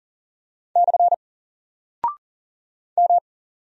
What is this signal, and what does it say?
Morse code, heard as a keyed tone, about 40 wpm, 700 Hz. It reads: / M
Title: Single Character (Letter:Number:Punctuation:Prosign) Alt 40wpm